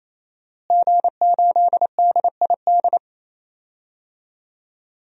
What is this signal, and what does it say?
Morse code, heard as a keyed tone, about 28 wpm, 700 Hz. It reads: G8DID
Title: DXCC-100 Call Signs - 5 Characters - Repeated 3x - 28wpm